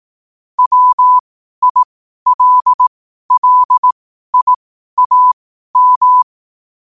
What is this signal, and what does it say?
Morse code, heard as a keyed tone, about 18 wpm, spaced as for 12 wpm, 1000 Hz. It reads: WILLIAM
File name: qotdfiles / 12wpm